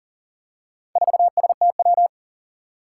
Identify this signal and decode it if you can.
Morse code, heard as a keyed tone, about 37 wpm, 700 Hz. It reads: 4STW
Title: Four Challenging Letter-Number Combinations 40wpm